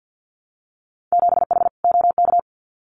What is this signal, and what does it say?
Morse code, heard as a keyed tone, about 50 wpm, 700 Hz. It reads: 7H OK
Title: Winter Field Day Exchanges Encore 50wpm